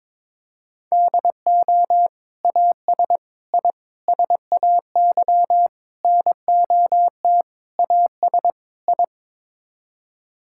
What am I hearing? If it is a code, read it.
Morse code, heard as a keyed tone, about 22 wpm, 700 Hz. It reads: DO AS I SAY NOT AS I
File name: practice / 22wpm / Sets of 7 Words - 22wpm